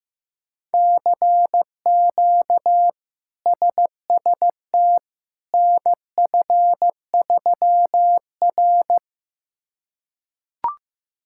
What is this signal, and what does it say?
Morse code, heard as a keyed tone, about 15 wpm, 700 Hz. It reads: CQ SST NF3R
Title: K1USN SST QSO Practice - 15wpm